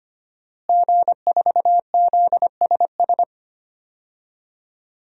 Morse code, heard as a keyed tone, about 25 wpm, 700 Hz. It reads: G4ZSS
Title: DXCC-100 Call Signs - 5 Characters - Repeated 2x - 25wpm